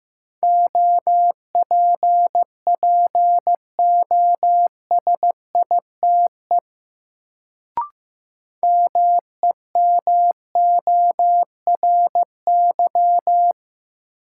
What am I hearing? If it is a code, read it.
Morse code, heard as a keyed tone, about 15 wpm, 700 Hz. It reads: OPPOSITE MEMORY
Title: Sentences from Top 1300 Words - New Word Drill - 15wpm